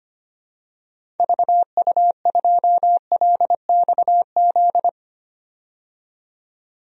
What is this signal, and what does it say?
Morse code, heard as a keyed tone, about 25 wpm, 700 Hz. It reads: VU2LXZ
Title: DXCC-100 Call Signs - 6 Characters - 25wpm